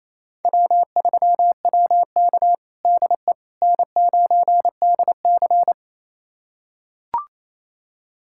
Morse code, heard as a keyed tone, about 28 wpm, 700 Hz. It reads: W3WK DE N9DC
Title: USA Callsign DE Callsign - 4 Characters - 28wpm